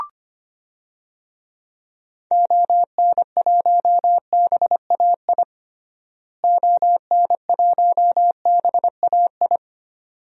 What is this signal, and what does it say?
Morse code, heard as a keyed tone, about 25 wpm, 700 Hz. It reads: ON1BAI ON1BAI
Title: DXCC-100 Call Signs - 6 Characters - Repeated 2x - 25wpm